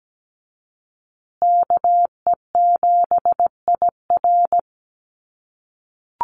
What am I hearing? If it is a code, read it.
Morse code, heard as a keyed tone, about 17 wpm, 700 Hz. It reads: KE7IR
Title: USA Call Signs Repeated 2x 17wpm